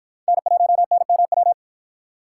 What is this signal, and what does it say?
Morse code, heard as a keyed tone, about 46 wpm, 700 Hz. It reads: N1NMW